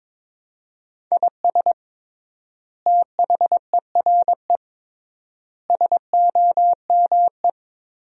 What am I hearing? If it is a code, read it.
Morse code, heard as a keyed tone, about 22 wpm, 700 Hz. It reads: IS THERE SOME
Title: 3x Word-Spacing - Sentences from Top 200 Words - 22wpm